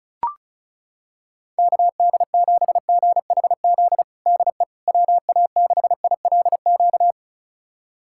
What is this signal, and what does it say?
Morse code, heard as a keyed tone, about 36 wpm, 700 Hz. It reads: KD7GHZ DE WA6ILQ
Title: USA Callsign DE Callsign - 6 Characters - 35wpm